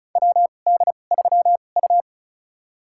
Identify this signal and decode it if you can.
Morse code, heard as a keyed tone, about 34 wpm, spaced as for 25 wpm, 700 Hz. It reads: WD3U
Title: USA Call Signs - 4 Characters - 2x Character Spacing - 35wpm